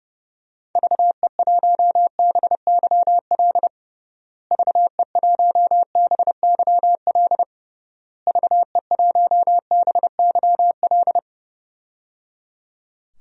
Morse code, heard as a keyed tone, about 30 wpm, 700 Hz. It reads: VE1BYL VE1BYL VE1BYL
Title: DXCC-100 Call Signs - 6 Characters - Repeated 3x - 30wpm